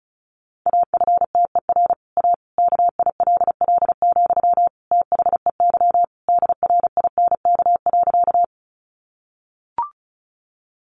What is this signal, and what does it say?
Morse code, heard as a keyed tone, about 34 wpm, 700 Hz. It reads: AFTER A KILL, THEY DRINK.
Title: Sentences from Top 1500 Words 35wpm